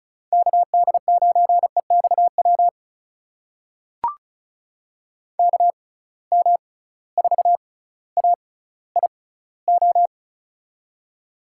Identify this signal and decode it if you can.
Morse code, heard as a keyed tone, about 34 wpm, spaced as for 18 wpm, 700 Hz. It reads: KD9EXW KM4AIO
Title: USA Call Signs - 6 Characters - 6x Character Spacing - 35wpm